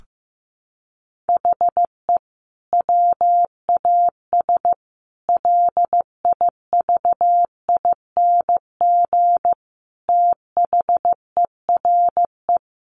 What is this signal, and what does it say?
Morse code, heard as a keyed tone, about 15 wpm, 700 Hz. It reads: HE WAS LIVING THERE
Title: Sentences from Top 1100 Words 15wpm